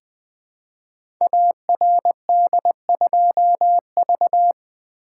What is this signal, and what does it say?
Morse code, heard as a keyed tone, about 20 wpm, 700 Hz. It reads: ARD2V
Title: Code Groups - 5 Letter:Numbers - 20wpm